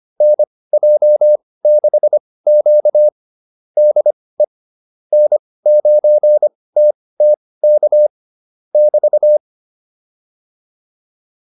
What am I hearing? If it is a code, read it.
Morse code, heard as a keyed tone, about 25 wpm, spaced as for 18 wpm, 600 Hz. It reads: NJ6Q DE N9TTK <BT>